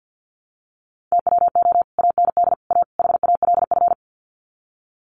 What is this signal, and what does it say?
Morse code, heard as a keyed tone, about 50 wpm, 700 Hz. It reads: TWO AND A HALF
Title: Sets of 4 Words 50wpm